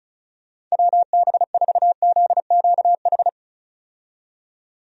Morse code, heard as a keyed tone, about 34 wpm, 700 Hz. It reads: WB4ZQH
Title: USA Call Signs - 6 Characters - 35wpm